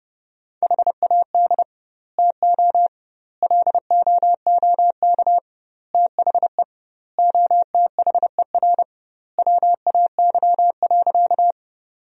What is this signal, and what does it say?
Morse code, heard as a keyed tone, about 30 wpm, 700 Hz. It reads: HAD TO LOOK THE OTHER WAY.
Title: 2x Word-Spacing - Sentences from Top 100 Words - 30wpm